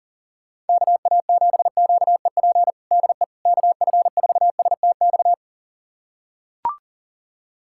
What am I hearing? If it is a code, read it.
Morse code, heard as a keyed tone, about 37 wpm, 700 Hz. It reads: KA7QEP DE KF4STX
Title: USA Callsign DE Callsign 40wpm